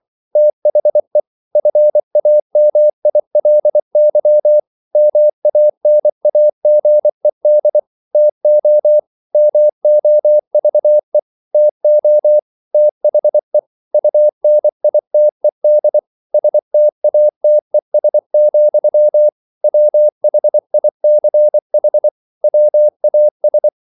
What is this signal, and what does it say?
Morse code, heard as a keyed tone, about 24 wpm, 600 Hz. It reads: THE FAMILY MANAGED TO MOVE TO THE UNITED STATES, WHICH WAS